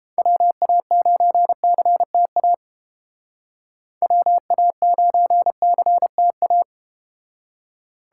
Morse code, heard as a keyed tone, about 30 wpm, 700 Hz. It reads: WA9CTA WA9CTA